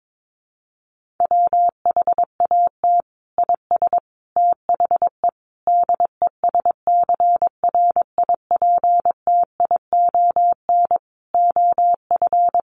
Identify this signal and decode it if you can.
Morse code, heard as a keyed tone, about 22 wpm, 700 Hz. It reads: WHAT IS THE DESCRIPTION OF
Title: Sentences from Top 1300 Words 22wpm